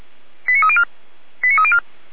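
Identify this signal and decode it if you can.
Five-tone selective call (ZVEI), heard as quick runs of five tones, tones of about 70 ms. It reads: 89304 89304
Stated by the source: Five-tone ZVEI selective call received on an SDR, recorded by fire brigade Feuerwehr Orlamünde.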